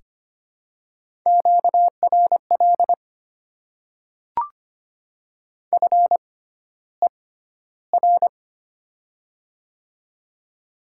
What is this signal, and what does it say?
Morse code, heard as a keyed tone, about 25 wpm, spaced as for 13 wpm, 700 Hz. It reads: QRL FER
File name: Top 20 QSO Elements - 6x Character Spacing - 25wpm